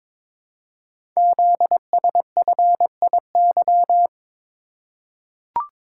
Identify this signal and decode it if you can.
Morse code, heard as a keyed tone, about 22 wpm, 700 Hz. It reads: ZSFIY
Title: Code Groups - 5 Letters - 22wpm